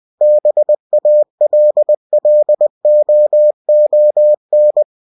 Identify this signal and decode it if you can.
Morse code, heard as a keyed tone, about 20 wpm, 600 Hz. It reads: BALLOON
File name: cw-books / winnie-the-pooh / winnie-20wpm-page-0004